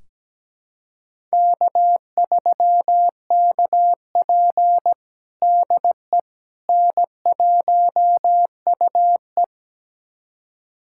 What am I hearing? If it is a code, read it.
Morse code, heard as a keyed tone, about 17 wpm, 700 Hz. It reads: K3KP DE N1UE
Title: USA Callsign DE Callsign - 4 Characters - 17wpm